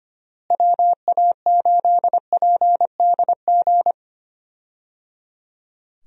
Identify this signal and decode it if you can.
Morse code, heard as a keyed tone, about 25 wpm, 700 Hz. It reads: WA8PDG
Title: USA Call Signs - 6 Characters - Repeated 3x - 25wpm